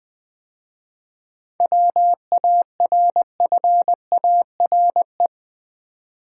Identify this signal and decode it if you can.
Morse code, heard as a keyed tone, about 20 wpm, 700 Hz. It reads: WARFARE
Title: 7 Letter Words - First 3 Letters Given - 20wpm